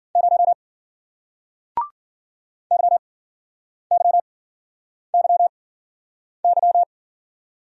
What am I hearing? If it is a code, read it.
Morse code, heard as a keyed tone, about 54 wpm, spaced as for 9 wpm, 700 Hz. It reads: 0 YYYY